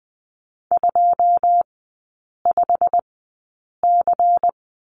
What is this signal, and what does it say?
Morse code, heard as a keyed tone, about 20 wpm, spaced as for 8 wpm, 700 Hz. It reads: 25C
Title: 093 Lesson 16e - Letter c - Three Characters 20wpm